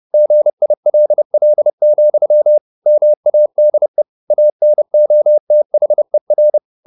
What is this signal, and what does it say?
Morse code, heard as a keyed tone, about 30 wpm, 600 Hz. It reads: GILL, MADE ANOTHER